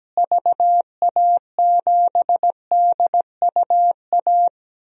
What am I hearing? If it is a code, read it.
Morse code, heard as a keyed tone, about 17 wpm, 700 Hz. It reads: VA7DUA
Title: DXCC-100 Call Signs - 6 Characters - Repeated 2x - 17wpm